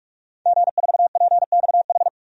Morse code, heard as a keyed tone, about 46 wpm, 700 Hz. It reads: G4PXH
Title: DXCC-100 Call Signs - 5 Characters - 45wpm